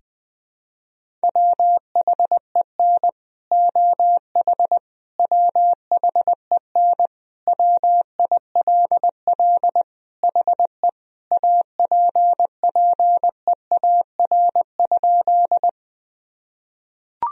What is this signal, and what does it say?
Morse code, heard as a keyed tone, about 20 wpm, 700 Hz. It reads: WHEN OH WHEN WILL HE APPEAR?
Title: Sentences from Top 1000 Words - Review Part 2 - 20wpm